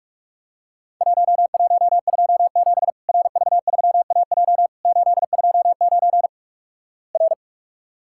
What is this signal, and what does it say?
Morse code, heard as a keyed tone, about 48 wpm, 700 Hz. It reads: 1127 RV3AJ 829 R
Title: Worked All Europe DX Contest - QTC Exchange - 45wpm